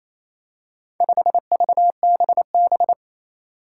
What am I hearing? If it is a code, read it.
Morse code, heard as a keyed tone, about 28 wpm, 700 Hz. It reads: 5VBB
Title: Four Challenging Letter-Number Combinations 28wpm